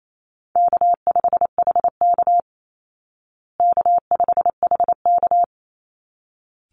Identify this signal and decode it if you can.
Morse code, heard as a keyed tone, about 28 wpm, 700 Hz. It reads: K5HK K5HK
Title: Easy Call Signs Repeated 2x - 28wpm